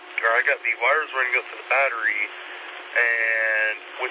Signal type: Emergency Alert System (SAME) tones